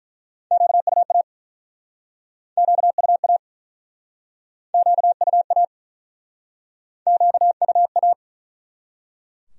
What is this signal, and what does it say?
Morse code, heard as a keyed tone, about 49 wpm, 700 Hz. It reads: QUA QUA QUA QUA